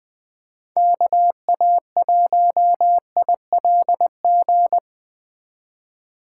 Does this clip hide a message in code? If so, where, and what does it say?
Yes: Morse code from 0.8 to 4.8 s, heard as a keyed tone, about 20 wpm, 700 Hz. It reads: KA1ILG